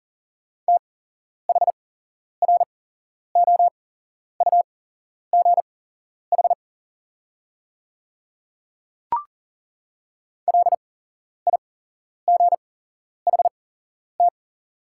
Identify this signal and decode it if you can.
Morse code, heard as a keyed tone, about 40 wpm, spaced as for 11 wpm, 700 Hz. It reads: THROUGH LIGHT
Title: 8x Character Spacing - Top 200 Words - 40wpm